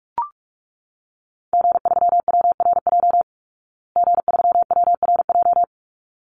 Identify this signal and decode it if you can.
Morse code, heard as a keyed tone, about 44 wpm, 700 Hz. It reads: G3WRJ G3WRJ
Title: Call Signs Repeated 3x - 45wpm